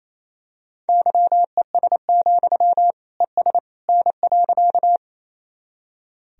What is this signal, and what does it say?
Morse code, heard as a keyed tone, about 28 wpm, 700 Hz. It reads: YES, ES N.
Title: Sentences from Top 500 Words and Abbreviations 28wpm